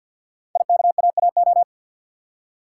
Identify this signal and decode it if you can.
Morse code, heard as a keyed tone, about 50 wpm, 700 Hz. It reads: IKAAO
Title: Code Groups - 5 Letters - 50wpm